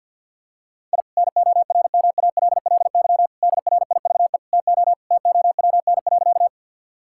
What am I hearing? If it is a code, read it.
Morse code, heard as a keyed tone, about 50 wpm, 700 Hz. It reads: I NORMALLY DRIVE TO TOWN.
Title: Sentences from Top 1000 Words - Review Part 6 - 50wpm